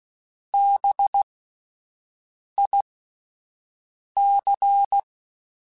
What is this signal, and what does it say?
Morse code, heard as a keyed tone, about 16 wpm, spaced as for 5 wpm, 800 Hz. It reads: BIC
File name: Spaced Alphas / Spaced Alphas mp3s / Bs sp @10